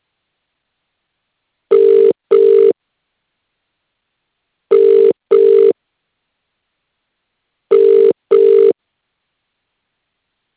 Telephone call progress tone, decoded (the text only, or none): ringback tone